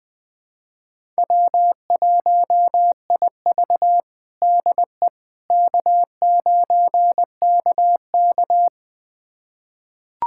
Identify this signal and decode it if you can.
Morse code, heard as a keyed tone, about 21 wpm, 700 Hz. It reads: W1IV DE K9KK
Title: USA Callsign DE Callsign - 4 Characters - 20wpm